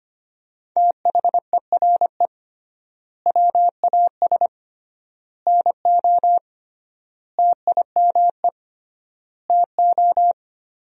Morse code, heard as a keyed tone, about 25 wpm, 700 Hz. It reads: THERE WAS NO TIME TO
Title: 3x Word-Spacing - Sentences from Top 400 Words - 25wpm